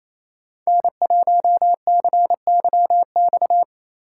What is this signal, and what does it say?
Morse code, heard as a keyed tone, about 34 wpm, 700 Hz. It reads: N1CYX
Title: Speed Racing - USA 1x3 Call Signs 28wpm